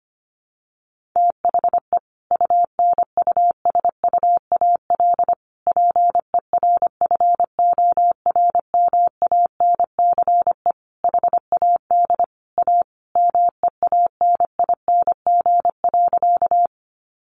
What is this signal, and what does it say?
Morse code, heard as a keyed tone, about 25 wpm, 700 Hz. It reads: THE UNUSUAL PERFORMANCE HAD A MEANING.
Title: Sentences from Top 1400 Words 25wpm